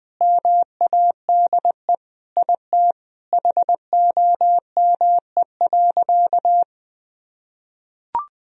Morse code, heard as a keyed tone, about 20 wpm, 700 Hz. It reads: MADE IT HOME.
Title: Sentences from Top 1000 Words - Review Part 2 - 20wpm